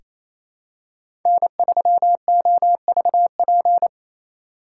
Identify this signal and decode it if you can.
Morse code, heard as a keyed tone, about 28 wpm, 700 Hz. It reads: N3OVP
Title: USA Call Signs - 3x Character Spacing - 28wpm